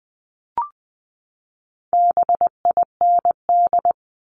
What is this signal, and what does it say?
Morse code, heard as a keyed tone, about 20 wpm, 700 Hz. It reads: BIND
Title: 260 Lesson 42e - Words 20-20wpm